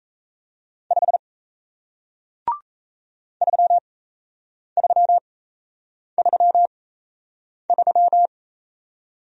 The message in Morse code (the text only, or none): 5 3333